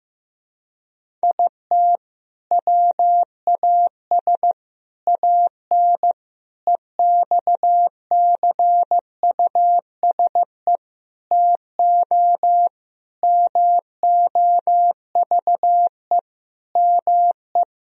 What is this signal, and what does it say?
Morse code, heard as a keyed tone, about 16 wpm, 700 Hz. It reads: IT WAS AN EXCUSE TO MOVE ME
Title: Sentences from Top 1000 Words - Review Part 1 - 15wpm